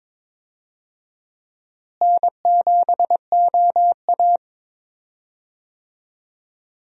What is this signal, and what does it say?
Morse code, heard as a keyed tone, about 22 wpm, 700 Hz. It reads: N7OA
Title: USA Call Signs - 4 Characters - Repeated 3x - 22wpm